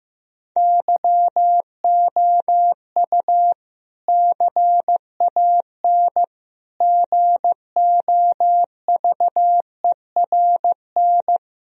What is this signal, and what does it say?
Morse code, heard as a keyed tone, about 15 wpm, 700 Hz. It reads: YOU CAN GOVERN
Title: Sentences from Top 1000 Words - Review Part 2 - 15wpm